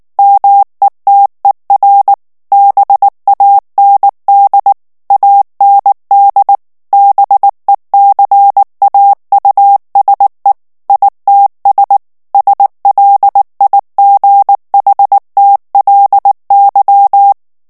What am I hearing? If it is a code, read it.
Morse code, heard as a keyed tone, about 19 wpm, 800 Hz. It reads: METER BAND AND BECAUSE ITS SLIGHTLY
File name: ANGLAIS texte clair 3 vitesse 18 mots minute